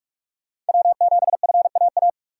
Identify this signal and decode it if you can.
Morse code, heard as a keyed tone, about 44 wpm, 700 Hz. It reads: W7FAA
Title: USA Call Signs - 5 Characters - 2x Character Spacing - 45wpm